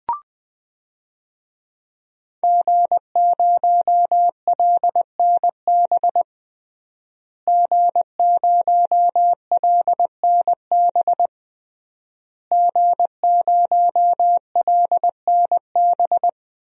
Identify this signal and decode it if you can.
Morse code, heard as a keyed tone, about 21 wpm, 700 Hz. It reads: G0LNB G0LNB G0LNB
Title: DXCC-100 Call Signs - 5 Characters - Repeated 3x - 20wpm